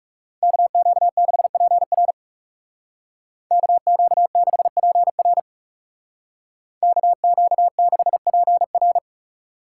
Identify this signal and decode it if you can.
Morse code, heard as a keyed tone, about 35 wpm, 700 Hz. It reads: KQ6PR KQ6PR KQ6PR
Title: Speed Racing - USA 2x2 Call Signs 30wpm